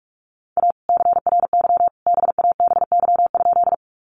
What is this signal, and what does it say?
Morse code, heard as a keyed tone, about 44 wpm, 700 Hz. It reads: A CRY BABY?
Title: Sentences from Top 1300 Words 45wpm